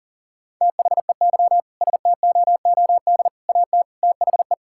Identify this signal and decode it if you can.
Morse code, heard as a keyed tone, about 40 wpm, 700 Hz. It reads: THEY STOOD AT THE
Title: Sentences from Top 1600 Words 40wpm